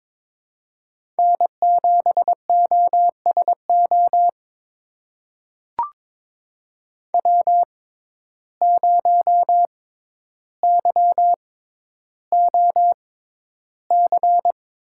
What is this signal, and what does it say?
Morse code, heard as a keyed tone, about 22 wpm, spaced as for 11 wpm, 700 Hz. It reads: N7OSO W0YOC
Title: USA Call Signs - 5 Characters - 6x Character Spacing - 22wpm